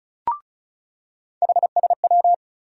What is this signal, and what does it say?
Morse code, heard as a keyed tone, about 34 wpm, 700 Hz. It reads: HSW
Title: USA Call Sign Postfixes - 3 Letters 35wpm